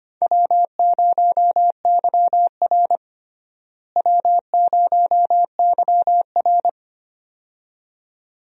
Morse code, heard as a keyed tone, about 25 wpm, 700 Hz. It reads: W0YR W0YR
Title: USA Call Signs - 4 Characters - Repeated 2x - 25wpm